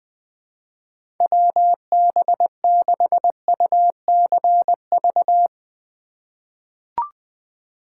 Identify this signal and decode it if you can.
Morse code, heard as a keyed tone, about 20 wpm, 700 Hz. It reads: WB6UCV
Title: USA Call Signs - 6 Characters - 3x Character Spacing - 20wpm